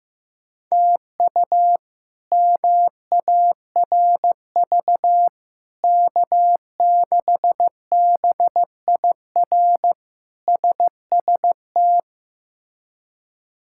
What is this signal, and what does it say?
Morse code, heard as a keyed tone, about 16 wpm, 700 Hz. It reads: TU MARV K6BIR SST
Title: K1USN SST QSO Practice - 15wpm